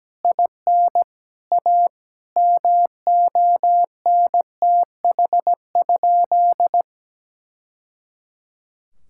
Morse code, heard as a keyed tone, about 17 wpm, 700 Hz. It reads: IN A MONTH?
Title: Sentences from Top 1000 Words - Review Part 6 - 17wpm